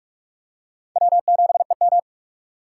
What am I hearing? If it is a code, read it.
Morse code, heard as a keyed tone, about 46 wpm, 700 Hz. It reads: W7EM
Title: USA Call Signs - 4 Characters - 4x Character Spacing - 45wpm